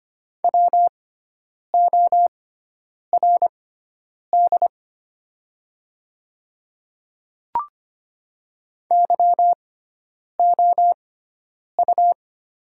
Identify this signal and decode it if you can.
Morse code, heard as a keyed tone, about 25 wpm, spaced as for 9 wpm, 690 Hz. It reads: WORD YOU
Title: 6x Character Spacing - Top 100 Words - 25wpm